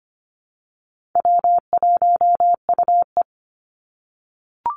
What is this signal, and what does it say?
Morse code, heard as a keyed tone, about 25 wpm, 700 Hz. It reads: W1UE
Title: Call Signs - 25wpm